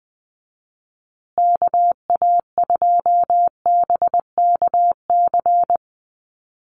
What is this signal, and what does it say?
Morse code, heard as a keyed tone, about 19 wpm, 700 Hz. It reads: KA2BKC
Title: USA Call Signs Repeated 3x 20wpm